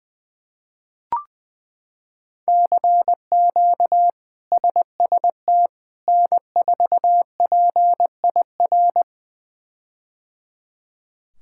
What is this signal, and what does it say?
Morse code, heard as a keyed tone, about 21 wpm, 700 Hz. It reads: CQ SST N4PIR
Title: K1USN SST QSO Practice - 20wpm